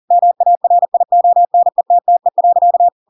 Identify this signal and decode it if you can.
Morse code, heard as a keyed tone, about 40 wpm, 700 Hz. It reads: MARIONETTE.
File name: Pinocchio / Pinocchio ch7 40 40 ik1hge